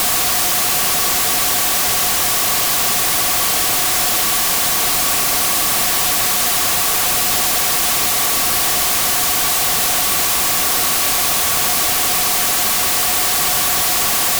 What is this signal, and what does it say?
Two-tone teleprinter signal, about 400 Hz shift.